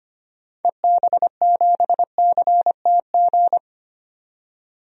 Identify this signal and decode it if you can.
Morse code, heard as a keyed tone, about 25 wpm, 700 Hz. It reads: EB7CTG
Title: DXCC-100 Call Signs - 6 Characters - 25wpm